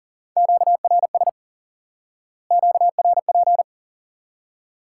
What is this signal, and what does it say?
Morse code, heard as a keyed tone, about 40 wpm, 700 Hz. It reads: QRS QRP
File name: IQE Course 3 - Lesson 89 - YAGI - Two Elements - 40wpm